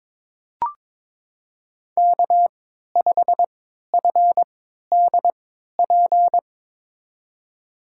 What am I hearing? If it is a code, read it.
Morse code, heard as a keyed tone, about 22 wpm, spaced as for 13 wpm, 700 Hz. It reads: K5FDP
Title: USA Call Signs - 5 Characters - 3x Character Spacing - 22wpm